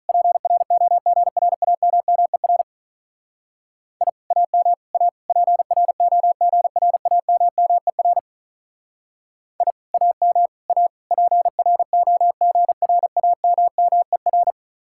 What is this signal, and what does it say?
Morse code, heard as a keyed tone, about 43 wpm, 700 Hz. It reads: PROGRAMMER I AM A PROGRAMMER I AM A PROGRAMMER
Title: Speed-Racing Common QSO Phrases Encore - 35wpm